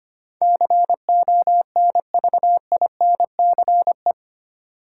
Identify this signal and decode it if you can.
Morse code, heard as a keyed tone, about 25 wpm, 700 Hz. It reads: CONVINCE
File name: Eight Letter Words - 25wpm